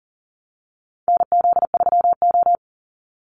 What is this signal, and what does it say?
Morse code, heard as a keyed tone, about 40 wpm, 700 Hz. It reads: NZ3O
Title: Call Signs Repeated 2x - 40wpm